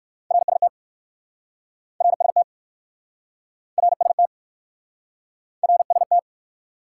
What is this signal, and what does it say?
Morse code, heard as a keyed tone, about 58 wpm, 700 Hz. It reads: RST RST RST RST